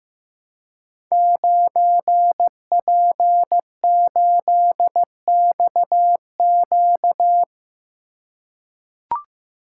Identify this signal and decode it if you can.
Morse code, heard as a keyed tone, about 15 wpm, 700 Hz. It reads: 9P8XQ
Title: Code Groups - 5 Letter:Numbers - 15wpm